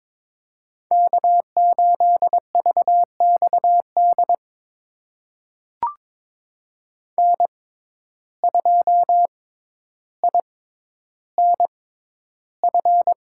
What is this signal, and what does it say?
Morse code, heard as a keyed tone, about 22 wpm, spaced as for 11 wpm, 700 Hz. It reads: K8VXD N2INF
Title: USA Call Signs - 5 Characters - 6x Character Spacing - 22wpm